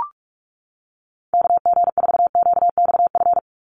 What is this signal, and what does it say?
Morse code, heard as a keyed tone, about 44 wpm, 700 Hz. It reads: KG4QXF
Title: USA Call Signs 45wpm